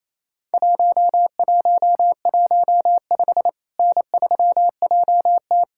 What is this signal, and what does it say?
Morse code, heard as a keyed tone, about 28 wpm, 700 Hz. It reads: 1115 N3JT
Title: Worked All Europe DX Contest - QTC Exchange - 28wpm